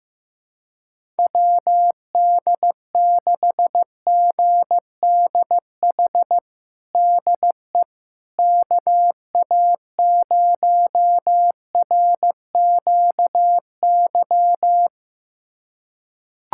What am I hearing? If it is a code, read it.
Morse code, heard as a keyed tone, about 16 wpm, 700 Hz. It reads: WD6GDH DE KA0RQY
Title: USA Callsign DE Callsign - 6 Characters - 15wpm